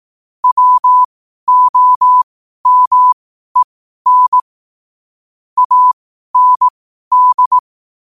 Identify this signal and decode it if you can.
Morse code, heard as a keyed tone, about 18 wpm, spaced as for 12 wpm, 1000 Hz. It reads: WOMEN AND
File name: qotdfiles / 12wpm